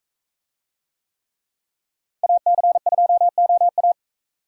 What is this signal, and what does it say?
Morse code, heard as a keyed tone, about 40 wpm, 700 Hz. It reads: AC2OA